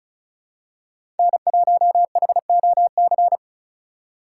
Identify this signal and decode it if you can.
Morse code, heard as a keyed tone, about 34 wpm, 700 Hz. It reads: N1HOC
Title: USA Call Signs - 5 Characters - 2x Character Spacing - 35wpm